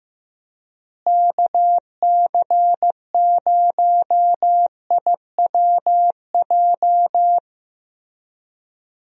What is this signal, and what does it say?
Morse code, heard as a keyed tone, about 15 wpm, 700 Hz. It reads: KC0IWJ
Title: USA Call Signs - 6 Characters - Repeated 3x - 15wpm